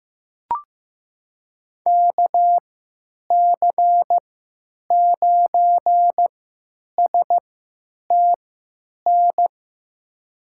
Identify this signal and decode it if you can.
Morse code, heard as a keyed tone, about 15 wpm, spaced as for 9 wpm, 700 Hz. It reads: KC9STN
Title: USA Call Signs - 6 Characters - 3x Character Spacing - 15wpm